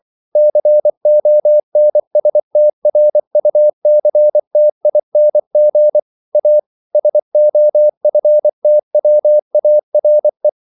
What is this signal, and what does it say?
Morse code, heard as a keyed tone, about 24 wpm, 600 Hz. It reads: CONSTRUCTING A SOFTWARE
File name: CW / Practice / EnglishQuotations0224